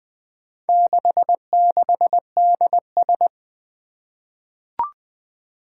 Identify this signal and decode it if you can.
Morse code, heard as a keyed tone, about 20 wpm, 700 Hz. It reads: 66DS
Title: Four Challenging Letter-Number Combinations 20wpm